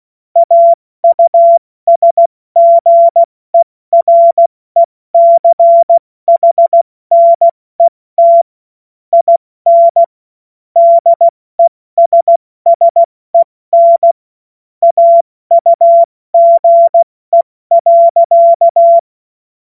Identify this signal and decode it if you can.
Morse code, heard as a keyed tone, about 16 wpm, 680 Hz. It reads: AUSGERECHNET IN DESSEN AUGE.
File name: Morsekurs / Witze / Witz 19 Dezember 2023 70 F80